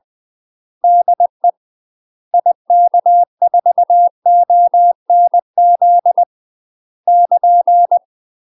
Morse code, heard as a keyed tone, about 20 wpm, 700 Hz. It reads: DE IK4ONZ <KN>
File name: UK MASTER QSOs TRAINING / 20WPM / UK MASTER QSOs 0005